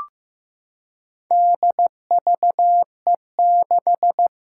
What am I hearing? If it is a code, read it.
Morse code, heard as a keyed tone, about 15 wpm, 700 Hz. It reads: DVE6
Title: Four Challenging Letter-Number Combinations 15wpm